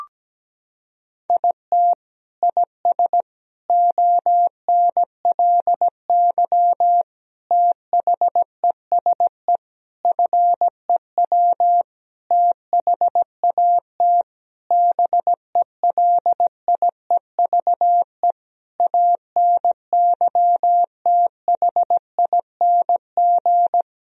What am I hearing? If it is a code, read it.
Morse code, heard as a keyed tone, about 17 wpm, 700 Hz. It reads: IT IS ONLY THESE FEW THAT BELIEVE ANYTHING
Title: Sentences from Top 1000 Words - Review Part 1 - 17wpm